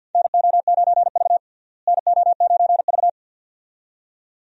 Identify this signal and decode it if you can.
Morse code, heard as a keyed tone, about 51 wpm, 700 Hz. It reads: NO9V NO9V
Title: USA Call Signs - 4 Characters - Repeated 2x - 50wpm